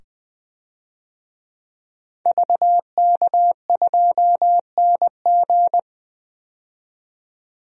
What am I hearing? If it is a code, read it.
Morse code, heard as a keyed tone, about 20 wpm, 700 Hz. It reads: VK2NG
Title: DXCC-100 Call Signs - 5 Characters - 20wpm